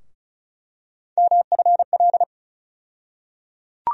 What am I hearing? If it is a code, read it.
Morse code, heard as a keyed tone, about 36 wpm, 700 Hz. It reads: MFL
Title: USA Call Sign Postfixes - 3 Letters 35wpm